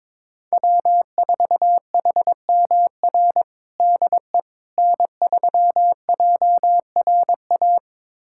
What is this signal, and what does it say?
Morse code, heard as a keyed tone, about 22 wpm, 700 Hz. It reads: W4HMR DE N3JRA
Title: USA Callsign DE Callsign - 5 Characters - 22wpm